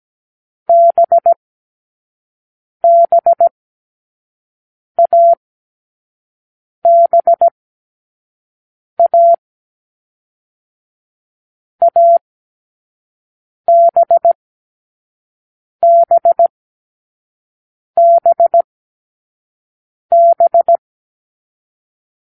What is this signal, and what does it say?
Morse code, heard as a keyed tone, about 17 wpm, spaced as for 5 wpm, 680 Hz. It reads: BBABA ABBBB